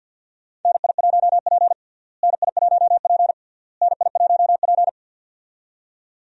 Morse code, heard as a keyed tone, about 51 wpm, 700 Hz. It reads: NI1P NI1P NI1P
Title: USA Call Signs - 4 Characters - Repeated 3x - 50wpm